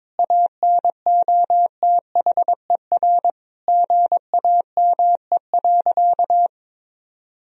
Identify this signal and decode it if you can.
Morse code, heard as a keyed tone, about 21 wpm, 700 Hz. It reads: ANOTHER GAME.
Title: Sentences from Top 1000 Words - Review Part 3 - 22wpm